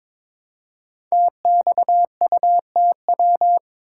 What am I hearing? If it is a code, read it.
Morse code, heard as a keyed tone, about 22 wpm, 700 Hz. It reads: TXUTW
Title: Code Groups - 5 Letters - 22wpm